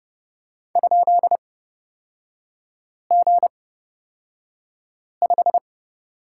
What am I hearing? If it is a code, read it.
Morse code, heard as a keyed tone, about 31 wpm, spaced as for 5 wpm, 700 Hz. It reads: ?G5
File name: Lesson 38e - Letter q - Three Characters 30wpm